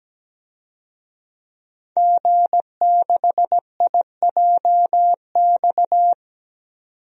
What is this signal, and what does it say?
Morse code, heard as a keyed tone, about 17 wpm, 700 Hz. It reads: G6IJX
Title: DXCC-100 Call Signs - 5 Characters - 17wpm